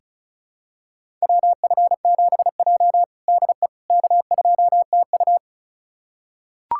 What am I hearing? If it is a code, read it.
Morse code, heard as a keyed tone, about 34 wpm, 700 Hz. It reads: WF7J DE K2TU
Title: USA Callsign DE Callsign - 4 Characters - 35wpm